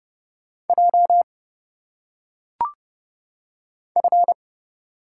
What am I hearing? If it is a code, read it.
Morse code, heard as a keyed tone, about 28 wpm, 700 Hz. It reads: J F
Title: Single Letters - Phonetic Pronunciation 30wpm